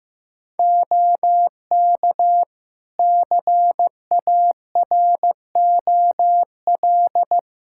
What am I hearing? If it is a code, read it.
Morse code, heard as a keyed tone, about 15 wpm, 700 Hz. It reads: OK CAROL
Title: Practice QSO - Exchange 4 - Ending - 15wpm